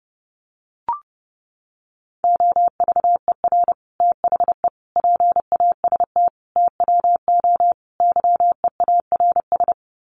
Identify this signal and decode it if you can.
Morse code, heard as a keyed tone, about 30 wpm, 700 Hz. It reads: OVER THE PAST TWO YEARS
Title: Sets of 5 Words Encore 30wpm